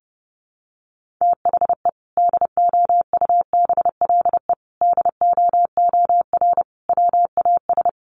Morse code, heard as a keyed tone, about 30 wpm, 700 Hz. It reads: THE DOUBLE DOOR WAS
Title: Sentences from Top 1400 Words 30wpm